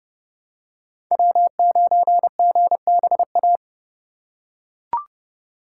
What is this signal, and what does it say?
Morse code, heard as a keyed tone, about 30 wpm, 700 Hz. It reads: W9GBA